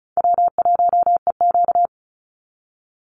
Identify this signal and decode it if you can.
Morse code, heard as a keyed tone, about 34 wpm, 700 Hz. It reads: W1EQ
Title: Easy Call Signs Repeated 3x - 35wpm